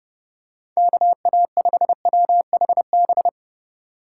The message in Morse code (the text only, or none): KA5WHB